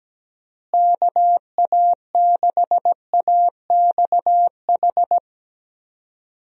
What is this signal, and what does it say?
Morse code, heard as a keyed tone, about 17 wpm, 700 Hz. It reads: KA6AXH
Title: USA Call Signs - 6 Characters - Repeated 3x - 17wpm